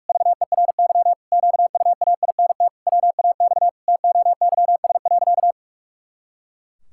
Morse code, heard as a keyed tone, about 46 wpm, 700 Hz. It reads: VERY QUAINT WAX TOYS.
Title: Pangram Challenge v2 - 45wpm